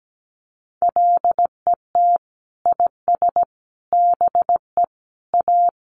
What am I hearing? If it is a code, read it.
Morse code, heard as a keyed tone, about 17 wpm, 700 Hz. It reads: LET IS BE A
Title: Sentences from Top 1200 Words 17wpm